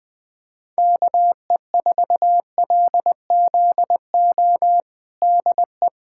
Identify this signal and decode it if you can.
Morse code, heard as a keyed tone, about 21 wpm, 700 Hz. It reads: KE4LZO DE
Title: USA Callsign DE Callsign - 6 Characters - 20wpm